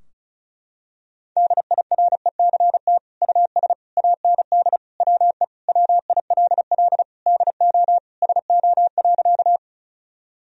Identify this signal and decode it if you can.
Morse code, heard as a keyed tone, about 34 wpm, 700 Hz. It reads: DIRECT US AND WE WILL DO SO.
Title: Sentences from Top 1000 Words - Review Part 2 - 35wpm